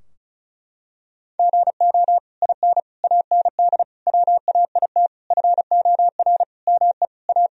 Morse code, heard as a keyed tone, about 36 wpm, 700 Hz. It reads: GO IN AND WAIT FOR ME A
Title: Sentences from Top 1000 Words - Review Part 2 - 35wpm